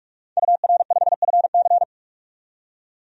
Morse code, heard as a keyed tone, about 41 wpm, 700 Hz. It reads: UR5FC
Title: DXCC-100 Call Signs - 5 Characters - 45wpm